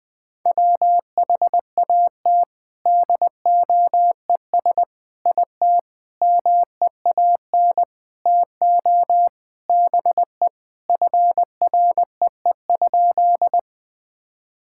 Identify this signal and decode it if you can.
Morse code, heard as a keyed tone, about 21 wpm, 700 Hz. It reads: WHAT DOES IT MEAN TO BE FREE?
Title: Sentences with 6-Word Phrases - 20wpm